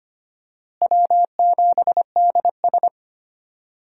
Morse code, heard as a keyed tone, about 25 wpm, 700 Hz. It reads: W7DS
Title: USA Call Signs - 4 Characters - Repeated 3x - 25wpm